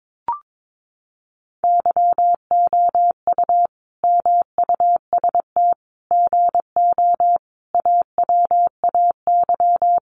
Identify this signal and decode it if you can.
Morse code, heard as a keyed tone, about 22 wpm, 700 Hz. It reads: YOU MUST GO AWAY
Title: Sentences from Top 1100 Words 22wpm